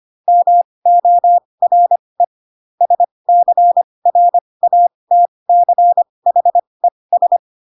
Morse code, heard as a keyed tone, about 25 wpm, spaced as for 20 wpm, 700 Hz. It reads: MORE SCRATCHES
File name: Pinocchio / Pinocchio ch2 15 25 ik1hge